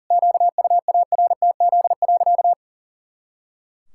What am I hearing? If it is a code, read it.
Morse code, heard as a keyed tone, about 40 wpm, 700 Hz. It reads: QUARTZ.
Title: Pangram Challenge v2 - 40wpm